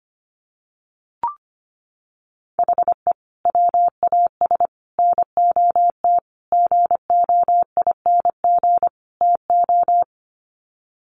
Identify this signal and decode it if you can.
Morse code, heard as a keyed tone, about 25 wpm, 700 Hz. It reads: HE WAS NOT GOING TO
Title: Sets of 5 Words 25wpm